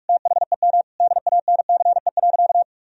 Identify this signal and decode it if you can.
Morse code, heard as a keyed tone, about 41 wpm, 700 Hz. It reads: THEM DANCE.
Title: Sentences from Top 1000 Words - Review Part 4 - 45wpm